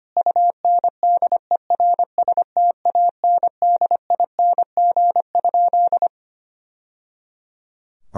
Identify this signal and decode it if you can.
Morse code, heard as a keyed tone, about 25 wpm, 700 Hz. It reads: UNDERSTANDING?
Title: Sentences from Top 1000 Words - Review Part 3 - 25wpm